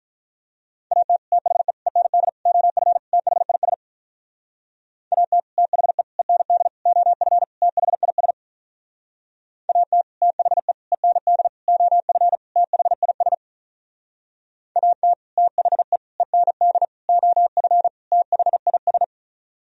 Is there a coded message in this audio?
Yes: Morse code from 0.9 to 19.0 s, heard as a keyed tone, about 45 wpm, 700 Hz. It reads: AT THE END OF THIS AT THE END OF THIS AT THE END OF THIS AT THE END OF THIS